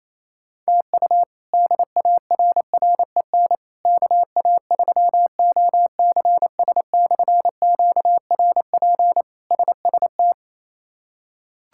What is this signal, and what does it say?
Morse code, heard as a keyed tone, about 28 wpm, 700 Hz. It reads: TU DARREN KA3OCS/QRP SST
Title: K1USN SST QSO Practice - 28wpm